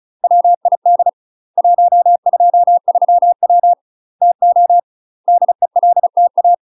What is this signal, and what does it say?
Morse code, heard as a keyed tone, about 37 wpm, 700 Hz. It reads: WID 123W TO DELTA
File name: UK MASTER QSOs TRAINING / 35WPM / UK MASTER QSOs 0001